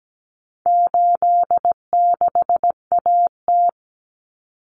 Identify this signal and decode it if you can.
Morse code, heard as a keyed tone, about 17 wpm, 700 Hz. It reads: 86AT
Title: Common Cut Numbers - 4-Digit 17wpm